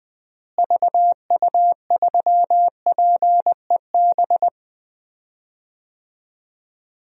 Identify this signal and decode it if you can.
Morse code, heard as a keyed tone, about 19 wpm, 700 Hz. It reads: VU3PEB
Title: DXCC-100 Call Signs - 6 Characters - Repeated 3x - 20wpm